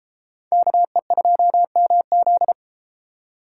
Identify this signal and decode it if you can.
Morse code, heard as a keyed tone, about 31 wpm, 700 Hz. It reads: KE2MZ